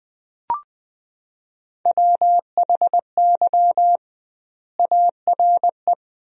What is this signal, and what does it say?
Morse code, heard as a keyed tone, about 21 wpm, 700 Hz. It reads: WHY ARE
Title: 2x Word-Spacing - Sentences from Top 200 Words - 20wpm